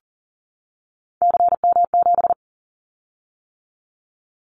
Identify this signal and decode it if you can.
Morse code, heard as a keyed tone, about 40 wpm, 700 Hz. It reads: CM7
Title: Callsign Prefixes for Top 100 DXCC - 40wpm